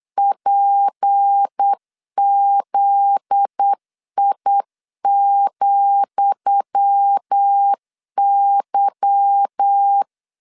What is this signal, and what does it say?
Morse code, heard as a keyed tone, about 8 wpm, 800 Hz. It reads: PZI,Y
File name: LEZIONI / 10 SIGNS / 10D COMA 8wpm 3749